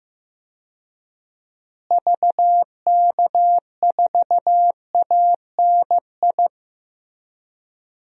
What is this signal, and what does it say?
Morse code, heard as a keyed tone, about 15 wpm, 700 Hz. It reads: VK4ANI
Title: DXCC-100 Call Signs - 6 Characters - Repeated 2x - 15wpm